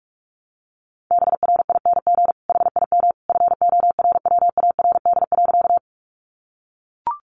Morse code, heard as a keyed tone, about 44 wpm, 700 Hz. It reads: BRING HIM FORWARD.
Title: Sentences from Top 1300 Words 45wpm